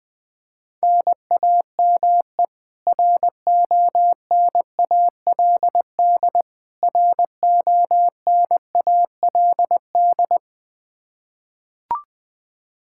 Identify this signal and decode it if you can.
Morse code, heard as a keyed tone, about 21 wpm, 700 Hz. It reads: NAME RONALD RONALD
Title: Practice QSO - Exchange 1 - Beginning - 20wpm